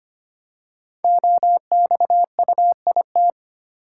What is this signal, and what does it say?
Morse code, heard as a keyed tone, about 25 wpm, 700 Hz. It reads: OXUIT
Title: Code Groups - 5 Letters - 25wpm